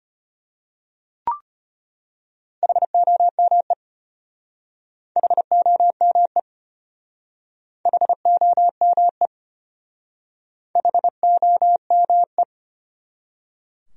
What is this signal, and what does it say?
Morse code, heard as a keyed tone, about 32 wpm, 700 Hz. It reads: HOME HOME HOME HOME